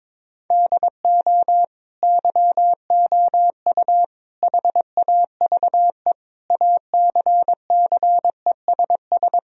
Morse code, heard as a keyed tone, about 22 wpm, 700 Hz. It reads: DO YOU HAVE ACCESS
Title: Sentences from Top 1000 Words - Review Part 4 - 22wpm